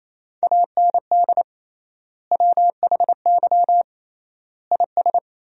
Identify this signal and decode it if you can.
Morse code, heard as a keyed tone, about 28 wpm, 700 Hz. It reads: AND WHY IS
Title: 3x Word-Spacing - Sentences from Top 200 Words - 28wpm